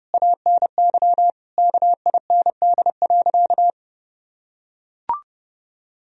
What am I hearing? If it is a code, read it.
Morse code, heard as a keyed tone, about 30 wpm, 700 Hz. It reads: ANY KIND.
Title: Sentences from Top 1000 Words - Review Part 2 - 30wpm